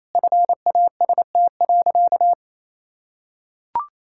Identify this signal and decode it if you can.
Morse code, heard as a keyed tone, about 28 wpm, 700 Hz. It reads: FAST.
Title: Sentences from Top 1600 Words 28wpm